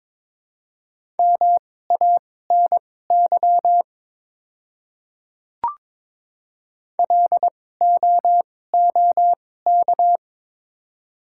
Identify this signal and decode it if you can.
Morse code, heard as a keyed tone, about 22 wpm, spaced as for 16 wpm, 700 Hz. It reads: MANY LOOK
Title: 2x Character Spacing - Top 100 Words - 22wpm